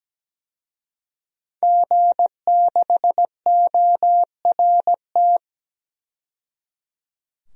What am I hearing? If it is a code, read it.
Morse code, heard as a keyed tone, about 17 wpm, 700 Hz. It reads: G6ORT
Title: DXCC-100 Call Signs - 5 Characters - 17wpm